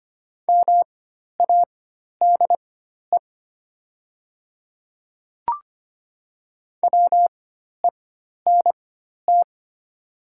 Morse code, heard as a keyed tone, about 25 wpm, spaced as for 12 wpm, 700 Hz. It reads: MADE WENT
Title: 4x Character Spacing - Top 300 Words - 25wpm